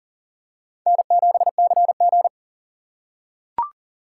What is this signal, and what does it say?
Morse code, heard as a keyed tone, about 40 wpm, 690 Hz. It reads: N7CG
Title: USA Call Signs - 6x Character Spacing - 40wpm